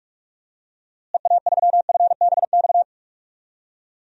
Morse code, heard as a keyed tone, about 47 wpm, 700 Hz. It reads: EA3FBX
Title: DXCC-100 Call Signs - 6 Characters - Repeated 2x - 45wpm